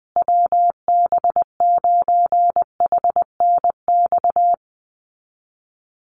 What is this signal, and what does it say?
Morse code, heard as a keyed tone, about 20 wpm, 700 Hz. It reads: WB9HNX
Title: USA Call Signs Repeated 3x 20wpm